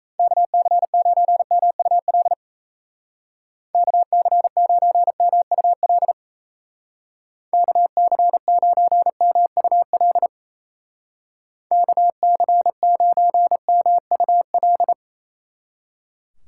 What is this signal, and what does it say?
Morse code, heard as a keyed tone, about 36 wpm, 700 Hz. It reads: KC9MUL KC9MUL KC9MUL KC9MUL